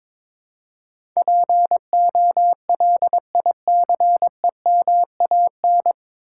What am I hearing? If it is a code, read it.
Morse code, heard as a keyed tone, about 22 wpm, 700 Hz. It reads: POLICEMAN
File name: 9 Letter Words - First 3 Letters Given - 22wpm